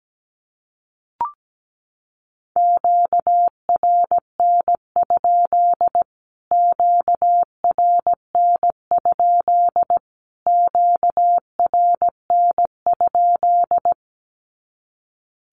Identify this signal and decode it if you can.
Morse code, heard as a keyed tone, about 17 wpm, 700 Hz. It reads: QRN? QRN? QRN?
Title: Q-Codes Repeated 3x 17wpm